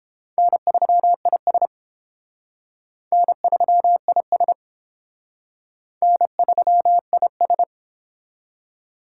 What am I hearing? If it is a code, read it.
Morse code, heard as a keyed tone, about 30 wpm, 700 Hz. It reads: N3IS N3IS N3IS